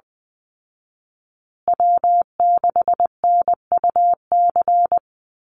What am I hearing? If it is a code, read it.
Morse code, heard as a keyed tone, about 20 wpm, 700 Hz. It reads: W6NUC
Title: Call Signs Repeated 2x - 20wpm